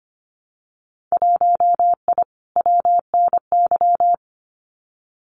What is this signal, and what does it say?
Morse code, heard as a keyed tone, about 25 wpm, 700 Hz. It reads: 1I WNY
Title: Winter Field Day Exchanges Encore 25wpm